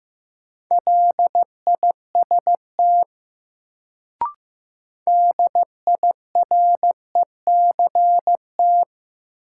Morse code, heard as a keyed tone, about 15 wpm, 700 Hz. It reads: LIST DIRECT
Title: Sentences from Top 400 Words - New Word Drill - 15wpm